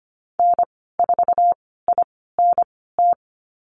Morse code, heard as a keyed tone, about 25 wpm, spaced as for 16 wpm, 700 Hz. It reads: N4INT
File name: Lesson 15g - Number 5 - Call Signs 25wpm